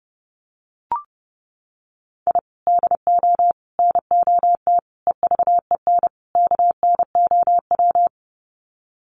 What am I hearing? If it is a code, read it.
Morse code, heard as a keyed tone, about 30 wpm, 700 Hz. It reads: I DO NOT EVEN KNOW
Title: Sets of 5 Words 30wpm